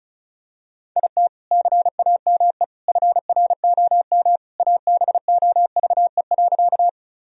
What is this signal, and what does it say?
Morse code, heard as a keyed tone, about 34 wpm, 700 Hz. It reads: IT CAME FROM ABOVE.
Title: Sentences from Top 1000 Words - Review Part 2 - 35wpm